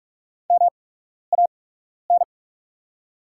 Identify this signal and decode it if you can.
Morse code, heard as a keyed tone, about 44 wpm, spaced as for 12 wpm, 690 Hz. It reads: MAN
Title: 8x Character Spacing - Top 300 Words - 45wpm